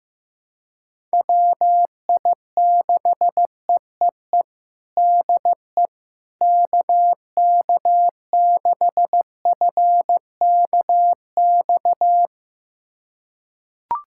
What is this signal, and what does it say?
Morse code, heard as a keyed tone, about 15 wpm, 700 Hz. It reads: WI6EEE DE KK6FKX
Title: USA Callsign DE Callsign - 6 Characters - 15wpm